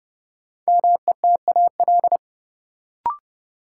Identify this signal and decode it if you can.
Morse code, heard as a keyed tone, about 30 wpm, 700 Hz. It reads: METAL
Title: Sentences from Top 1600 Words - New Word Drill Alt - 30wpm